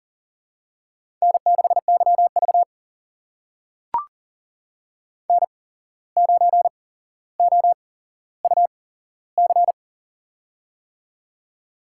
Morse code, heard as a keyed tone, about 38 wpm, spaced as for 16 wpm, 700 Hz. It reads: N6YV N9OUC
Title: USA Call Signs - 8x Character Spacing - 40wpm